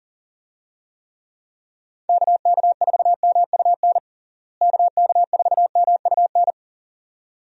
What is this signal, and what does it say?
Morse code, heard as a keyed tone, about 40 wpm, 700 Hz. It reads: KK4MUN KK4MUN
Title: USA Call Signs - 6 Characters - Repeated 2x - 40wpm